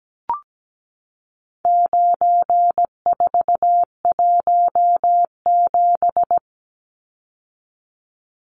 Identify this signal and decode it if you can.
Morse code, heard as a keyed tone, about 17 wpm, 700 Hz. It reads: 9417
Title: Numbers - 4-Digit 17wpm